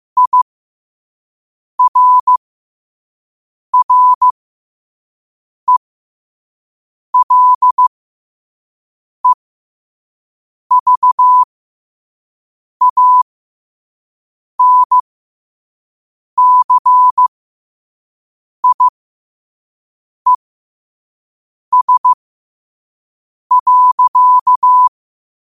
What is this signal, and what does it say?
Morse code, heard as a keyed tone, about 15 wpm, spaced as for 5 wpm, 1000 Hz. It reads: IRRELEVANCIES.
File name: 5wpm